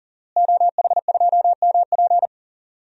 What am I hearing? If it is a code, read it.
Morse code, heard as a keyed tone, about 40 wpm, 700 Hz. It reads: OH2MP
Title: DXCC-100 Call Signs - 5 Characters - Repeated 2x - 40wpm